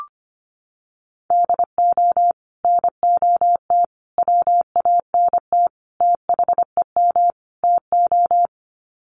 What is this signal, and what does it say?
Morse code, heard as a keyed tone, about 25 wpm, 700 Hz. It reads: DO NOT WANT THEM TO
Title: Sets of 5 Words Encore 25wpm